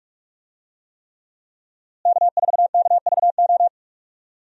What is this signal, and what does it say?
Morse code, heard as a keyed tone, about 46 wpm, 700 Hz. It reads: K4KVO
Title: USA Call Signs - 5 Characters - Repeated 3x - 45wpm